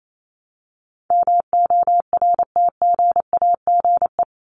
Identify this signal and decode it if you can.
Morse code, heard as a keyed tone, about 28 wpm, 700 Hz. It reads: MORTGAGE
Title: Top 1751-2000 Words 28wpm